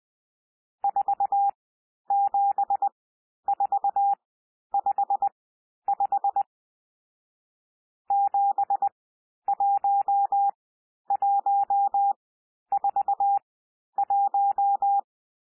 Morse code, heard as a keyed tone, about 20 wpm, spaced as for 11 wpm, 800 Hz. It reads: 47455 71141
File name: cwmp3 / N1 SOUND NUMBERS 20 wpm